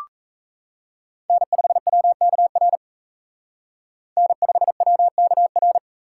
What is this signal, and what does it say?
Morse code, heard as a keyed tone, about 35 wpm, 700 Hz. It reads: N5WKR N5WKR